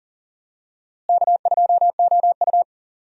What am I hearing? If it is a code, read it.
Morse code, heard as a keyed tone, about 40 wpm, 700 Hz. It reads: K2OU
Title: USA Call Signs - 4 Characters - Repeated 2x - 40wpm